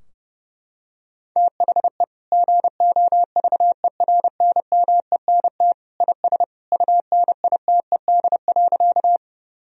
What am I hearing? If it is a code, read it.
Morse code, heard as a keyed tone, about 30 wpm, 700 Hz. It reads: THE GOVERNMENT IS UNITED.
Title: Sentences from Top 1000 Words - Review Part 4 - 30wpm